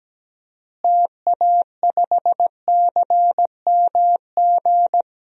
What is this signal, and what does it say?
Morse code, heard as a keyed tone, about 17 wpm, 700 Hz. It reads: TA5CMG
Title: DXCC-100 Call Signs - 6 Characters - Repeated 2x - 17wpm